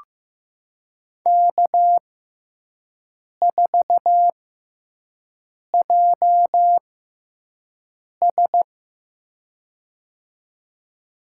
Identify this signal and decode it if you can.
Morse code, heard as a keyed tone, about 15 wpm, spaced as for 5 wpm, 700 Hz. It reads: K4JS
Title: USA Call Signs - 4 Characters - 6x Character Spacing - 15wpm